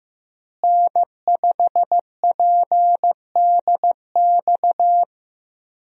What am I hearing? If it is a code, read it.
Morse code, heard as a keyed tone, about 15 wpm, 700 Hz. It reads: N5PDX
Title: USA Call Signs - 5 Characters - Repeated 3x - 15wpm